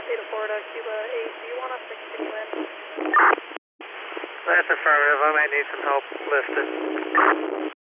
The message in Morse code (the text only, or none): none